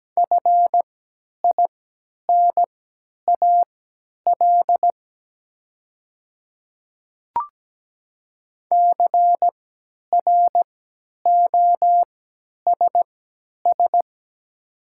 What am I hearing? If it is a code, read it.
Morse code, heard as a keyed tone, about 17 wpm, spaced as for 10 wpm, 700 Hz. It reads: FINAL CROSS
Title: 3x Character Spacing - Top 500 Words - 17wpm